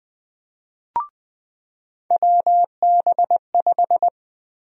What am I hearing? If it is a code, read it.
Morse code, heard as a keyed tone, about 20 wpm, 700 Hz. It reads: WB5
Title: USA Call Sign Prefixes - 20wpm